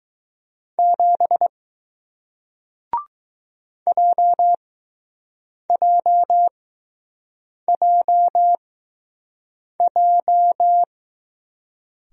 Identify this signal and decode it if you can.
Morse code, heard as a keyed tone, about 21 wpm, spaced as for 6 wpm, 700 Hz. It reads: 7 JJJJ